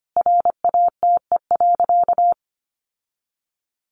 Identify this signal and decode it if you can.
Morse code, heard as a keyed tone, about 25 wpm, 700 Hz. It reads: RATE.
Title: Sentences from Top 900 Words 25wpm